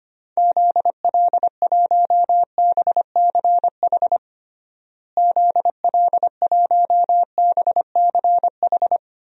Morse code, heard as a keyed tone, about 25 wpm, 700 Hz. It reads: ZL1BCH ZL1BCH
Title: DXCC-100 Call Signs - 6 Characters - Repeated 2x - 25wpm